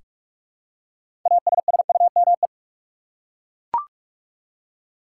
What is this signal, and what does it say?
Morse code, heard as a keyed tone, about 42 wpm, 700 Hz. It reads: ASSUME
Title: Six Letter Words - 45wpm